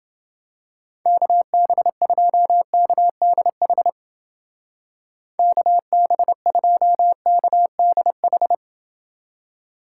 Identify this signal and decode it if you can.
Morse code, heard as a keyed tone, about 30 wpm, 700 Hz. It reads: KB2KDH KB2KDH